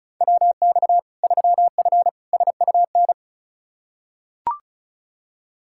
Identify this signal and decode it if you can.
Morse code, heard as a keyed tone, about 36 wpm, 700 Hz. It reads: WX 3F SUN
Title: Practice QSO - Exchange 2 - Info - 35wpm